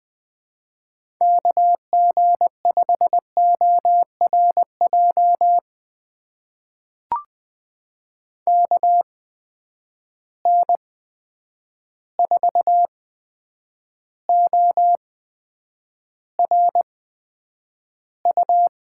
Morse code, heard as a keyed tone, about 20 wpm, spaced as for 9 wpm, 700 Hz. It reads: KG5ORJ KN4ORU
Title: USA Call Signs - 6 Characters - 8x Character Spacing - 20wpm